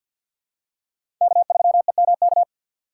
Morse code, heard as a keyed tone, about 50 wpm, 700 Hz. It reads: K3EMK
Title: USA Call Signs - 5 Characters - 4x Character Spacing - 50wpm